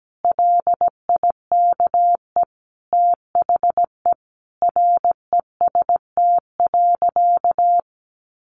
Morse code, heard as a keyed tone, about 17 wpm, 700 Hz. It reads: LIKE THE REST.
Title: Sentences from Top 1100 Words 17wpm